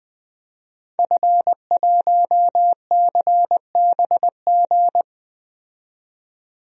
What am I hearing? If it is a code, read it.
Morse code, heard as a keyed tone, about 20 wpm, 700 Hz. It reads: F1CBG
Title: DXCC-100 Call Signs - 5 Characters - 20wpm